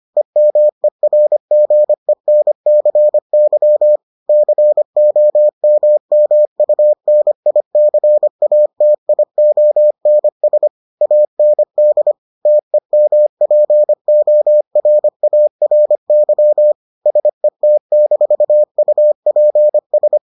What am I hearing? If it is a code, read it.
Morse code, heard as a keyed tone, about 25 wpm, 600 Hz. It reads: EMERGENCY COMMUNICATIONS AND TEMPORARY SET-UPS